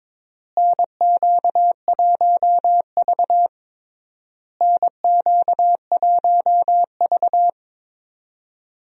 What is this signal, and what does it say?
Morse code, heard as a keyed tone, about 21 wpm, 700 Hz. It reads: NQ1V NQ1V
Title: USA Call Signs - 4 Characters - Repeated 3x - 22wpm